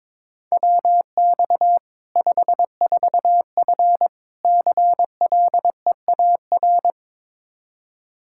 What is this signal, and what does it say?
Morse code, heard as a keyed tone, about 21 wpm, 700 Hz. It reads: WX 54F CLEAR
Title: K1USN SST QSO Practice 22wpm